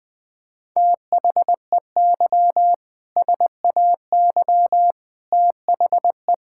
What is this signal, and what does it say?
Morse code, heard as a keyed tone, about 20 wpm, 700 Hz. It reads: THEY SAY THE
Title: Sentences from Top 1600 Words 20wpm